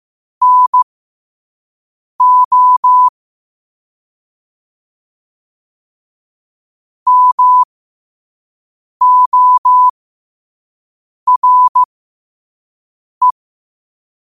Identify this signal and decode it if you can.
Morse code, heard as a keyed tone, about 15 wpm, spaced as for 5 wpm, 1000 Hz. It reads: NO MORE
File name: 5wpm